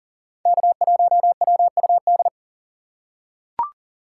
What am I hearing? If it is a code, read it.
Morse code, heard as a keyed tone, about 40 wpm, 700 Hz. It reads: K1WUD
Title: USA Call Signs - 6x Character Spacing - 40wpm